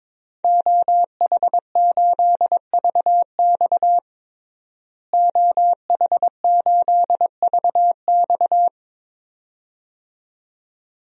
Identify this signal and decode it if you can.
Morse code, heard as a keyed tone, about 22 wpm, 700 Hz. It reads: OH8VX OH8VX
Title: DXCC-100 Call Signs - 5 Characters - Repeated 2x - 22wpm